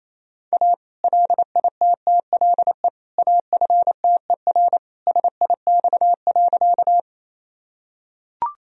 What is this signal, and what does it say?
Morse code, heard as a keyed tone, about 27 wpm, 700 Hz. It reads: A LITTLE AFTER SIX.
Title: Sentences from Top 1000 Words - Review Part 2 - 28wpm